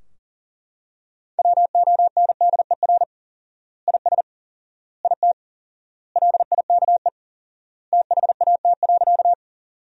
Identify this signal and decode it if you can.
Morse code, heard as a keyed tone, about 40 wpm, 700 Hz. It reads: WONDER IS IT LIKE THAT.
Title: 4x Word-Spacing - Sentences from Top 500 Words - 40wpm